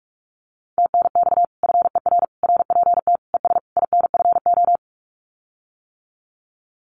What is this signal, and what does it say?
Morse code, heard as a keyed tone, about 46 wpm, 700 Hz. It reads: TNX FER RPT ES INFO
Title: Common QSO Phrases Encore 45wpm